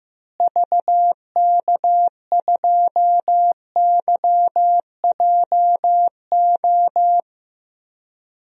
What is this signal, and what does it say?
Morse code, heard as a keyed tone, about 15 wpm, 700 Hz. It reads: VK2YJO
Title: DXCC-100 Call Signs - 6 Characters - Repeated 2x - 15wpm